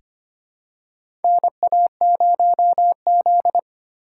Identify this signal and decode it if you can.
Morse code, heard as a keyed tone, about 25 wpm, 700 Hz. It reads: NA0Z
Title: USA Call Signs - 4 Characters - 4x Character Spacing - 25wpm